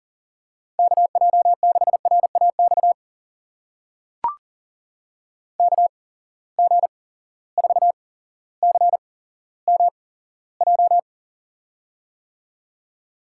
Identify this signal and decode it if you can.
Morse code, heard as a keyed tone, about 40 wpm, spaced as for 17 wpm, 700 Hz. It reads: KJ6RAX KG4CMJ
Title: USA Call Signs - 6 Characters - 8x Character Spacing - 40wpm